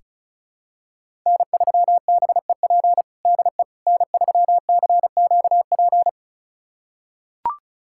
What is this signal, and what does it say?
Morse code, heard as a keyed tone, about 36 wpm, 700 Hz. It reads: N3BEP DE N3CQP
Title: USA Callsign DE Callsign - 5 Characters - 35wpm